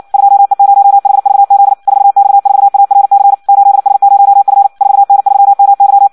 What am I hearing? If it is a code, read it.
Morse code, heard as a keyed tone, about 56 wpm, 800 Hz. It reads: 2E0HVZ 4X4ARC 8S0F 4N3MY
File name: Audio practica CW / qrq-tx-58 2